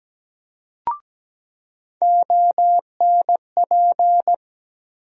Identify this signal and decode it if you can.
Morse code, heard as a keyed tone, about 17 wpm, 700 Hz. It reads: ONP
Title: USA Call Sign Postfixes - 3 Letters 17wpm